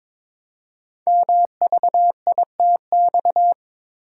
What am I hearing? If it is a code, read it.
Morse code, heard as a keyed tone, about 22 wpm, 700 Hz. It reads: MVITX
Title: Code Groups - 5 Letters - 22wpm